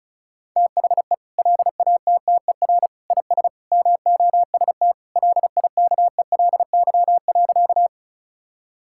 Morse code, heard as a keyed tone, about 36 wpm, 700 Hz. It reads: THE LATTER IS MOST LIKELY.
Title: Sentences from Top 1600 Words 35wpm